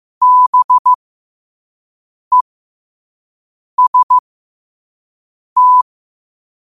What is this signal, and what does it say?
Morse code, heard as a keyed tone, about 15 wpm, spaced as for 5 wpm, 1000 Hz. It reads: BEST
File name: qotdfiles / 5wpm